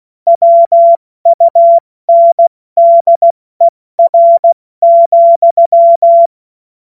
Morse code, heard as a keyed tone, about 16 wpm, 680 Hz. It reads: WUNDER,
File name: Morsekurs / Witze / Witz 21. Februar 2023 70 F80